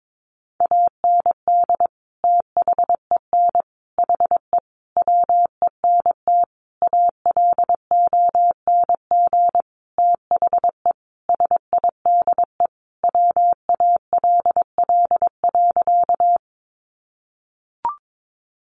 Morse code, heard as a keyed tone, about 22 wpm, 700 Hz. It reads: AND THEN HE WENT ALONG THE SIDE WALL.
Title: Sentences from Top 1100 Words 22wpm